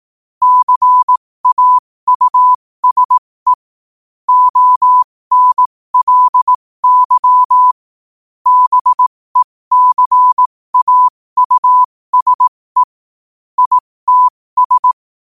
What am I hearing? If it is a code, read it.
Morse code, heard as a keyed tone, about 18 wpm, spaced as for 15 wpm, 1000 Hz. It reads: CAUSE ONLY BECAUSE ITS
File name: qotdfiles / 15wpm